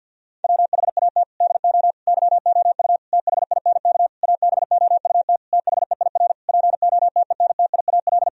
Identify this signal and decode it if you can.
Morse code, heard as a keyed tone, about 50 wpm, 700 Hz. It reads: WHAT DO YOU THINK ABOUT THEIR POTENTIAL
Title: Sentences from Top 1000 Words - Review Part 4 - 50wpm